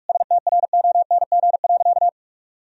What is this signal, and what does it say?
Morse code, heard as a keyed tone, about 41 wpm, 700 Hz. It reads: STRONG.
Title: Sentences from Top 1000 Words - Review Part 4 - 45wpm